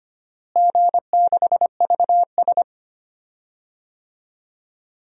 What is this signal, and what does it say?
Morse code, heard as a keyed tone, about 25 wpm, 700 Hz. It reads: G6VS
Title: DXCC-100 Call Signs - 4 Characters - Repeated 2x - 25wpm